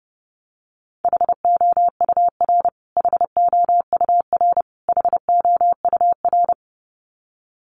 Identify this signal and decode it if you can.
Morse code, heard as a keyed tone, about 30 wpm, 700 Hz. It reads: HOUR HOUR HOUR
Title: Top 500 Words Repeated 30wpm